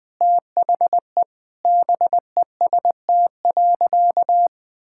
Morse code, heard as a keyed tone, about 20 wpm, 700 Hz. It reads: THE BEST.
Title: Sentences with 4-Word Phrases - 20wpm